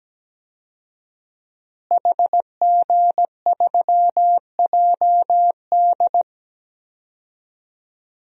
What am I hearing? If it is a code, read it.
Morse code, heard as a keyed tone, about 17 wpm, 700 Hz. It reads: HG3JD
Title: DXCC-100 Call Signs - 5 Characters - Repeated 3x - 17wpm